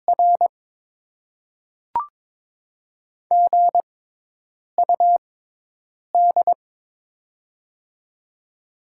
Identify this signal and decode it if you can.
Morse code, heard as a keyed tone, about 22 wpm, spaced as for 8 wpm, 700 Hz. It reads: R GUD
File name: Top 20 QSO Elements - 6x Character Spacing - 22wpm